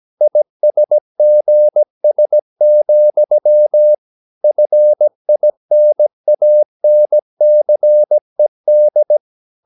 Morse code, heard as a keyed tone, about 17 wpm, 600 Hz. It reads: ISGS, FINANCED
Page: Romanen Never af Ken Follet på engelsk.